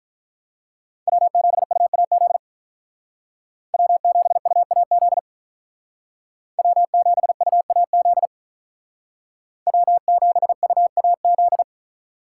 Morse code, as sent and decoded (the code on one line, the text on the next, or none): .-- --... ..- .- --.. / .-- --... ..- .- --.. / .-- --... ..- .- --.. / .-- --... ..- .- --..
W7UAZ W7UAZ W7UAZ W7UAZ